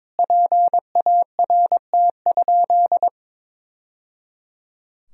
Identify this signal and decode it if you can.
Morse code, heard as a keyed tone, about 22 wpm, 700 Hz. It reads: PART?
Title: Sentences from Top 1000 Words - Review Part 4 - 22wpm